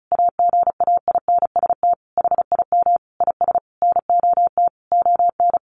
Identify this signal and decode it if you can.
Morse code, heard as a keyed tone, about 34 wpm, 700 Hz. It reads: AGAINST HIM IS NOT ON
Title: Sentences from Top 1400 Words 35wpm